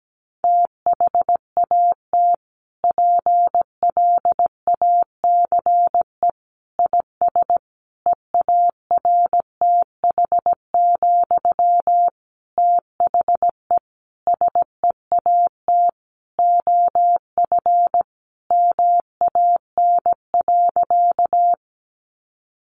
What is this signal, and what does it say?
Morse code, heard as a keyed tone, about 17 wpm, 700 Hz. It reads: THAT PLACE IS EARTH, THE SEAT OF MAN.
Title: Sentences from Top 1500 Words 17wpm